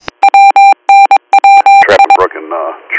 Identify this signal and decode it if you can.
Morse code, heard as a keyed tone, about 22 wpm, 790 Hz. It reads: WNWI